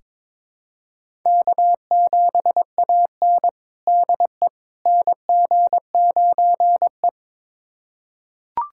Morse code, heard as a keyed tone, about 22 wpm, 700 Hz. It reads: K7AN DE NG9E
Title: USA Callsign DE Callsign - 4 Characters - 22wpm